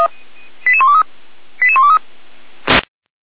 Five-tone selective call (ZVEI), heard as quick runs of five tones, tones of about 80 ms. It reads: 88123 88123